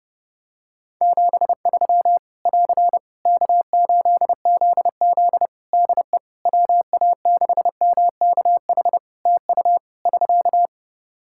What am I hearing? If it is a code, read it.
Morse code, heard as a keyed tone, about 30 wpm, 700 Hz. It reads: 73 <AR> K8ZZ DE WA6MKH TU <SK>
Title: K1USN SST QSO Practice 30wpm